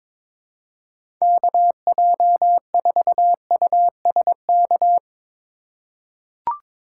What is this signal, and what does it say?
Morse code, heard as a keyed tone, about 22 wpm, 700 Hz. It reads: KJ4USK
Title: USA Call Signs - 6 Characters - 3x Character Spacing - 22wpm